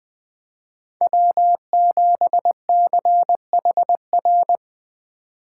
Morse code, heard as a keyed tone, about 21 wpm, 700 Hz. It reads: W7CHR